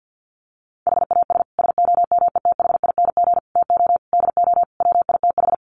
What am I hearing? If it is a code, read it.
Morse code, heard as a keyed tone, about 49 wpm, 700 Hz. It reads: HAS SOMETHING TO DO WITH
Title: Sets of 5 Words Encore 50wpm